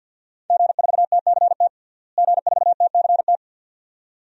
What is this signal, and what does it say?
Morse code, heard as a keyed tone, about 49 wpm, 700 Hz. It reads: G4TCT G4TCT
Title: DXCC-100 Call Signs - 5 Characters - Repeated 2x - 50wpm